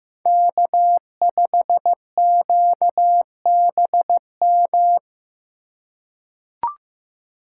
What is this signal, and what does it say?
Morse code, heard as a keyed tone, about 15 wpm, 700 Hz. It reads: K5QBM
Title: USA Call Signs - 5 Characters - 2x Character Spacing - 15wpm